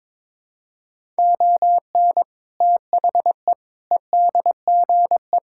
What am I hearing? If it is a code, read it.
Morse code, heard as a keyed tone, about 23 wpm, 700 Hz. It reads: ON THE EDGE
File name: practice / 22wpm / Sentences with 3-Word Phrases - 22wpm